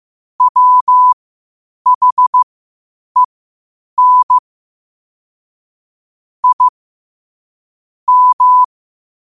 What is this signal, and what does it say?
Morse code, heard as a keyed tone, about 15 wpm, spaced as for 8 wpm, 1000 Hz. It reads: WHEN I M
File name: qotdfiles / 8wpm